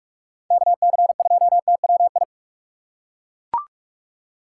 Morse code, heard as a keyed tone, about 41 wpm, 700 Hz. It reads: KC2TWI